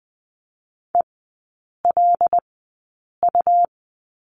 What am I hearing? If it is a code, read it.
Morse code, heard as a keyed tone, about 19 wpm, spaced as for 8 wpm, 700 Hz. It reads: ELU
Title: 100 Lesson 17e - Letter u - Three Characters 20wpm